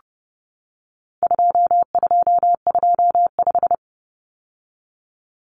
Morse code, heard as a keyed tone, about 30 wpm, 700 Hz. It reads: 2225
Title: Common Cut Numbers - 4-Digit 30wpm